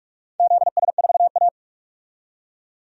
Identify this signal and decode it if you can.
Morse code, heard as a keyed tone, about 42 wpm, 700 Hz. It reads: ZS4A
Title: DXCC-100 Call Signs - 4 Characters - 45wpm